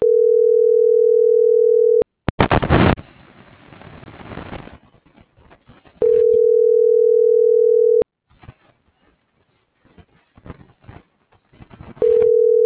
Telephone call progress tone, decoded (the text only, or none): ringback tone